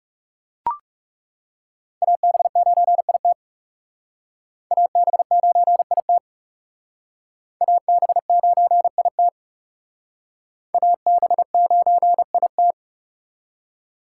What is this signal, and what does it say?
Morse code, heard as a keyed tone, about 37 wpm, 700 Hz. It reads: AB9IT AB9IT AB9IT AB9IT
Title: Speed Racing - USA 2x2 Call Signs 30wpm